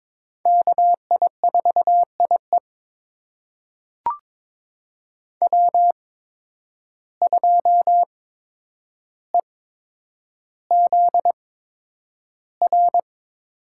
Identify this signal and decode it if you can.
Morse code, heard as a keyed tone, about 22 wpm, spaced as for 9 wpm, 700 Hz. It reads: KI4IE W2EZR
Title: USA Call Signs - 5 Characters - 8x Character Spacing - 22wpm